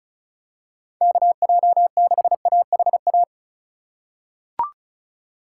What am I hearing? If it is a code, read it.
Morse code, heard as a keyed tone, about 36 wpm, 700 Hz. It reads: KJ6AHA
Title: USA Call Signs - 2x Character Spacing - 35wpm